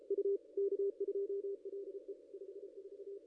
Morse code, heard as a keyed tone, about 34 wpm, 410 Hz. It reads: U K2LE 54
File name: contests / ss-cw-k5tr-op-n5rz -0500